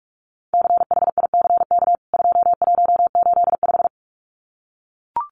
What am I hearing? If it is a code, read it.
Morse code, heard as a keyed tone, about 44 wpm, 700 Hz. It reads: CHICK 2185
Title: CWT Exchange 45wpm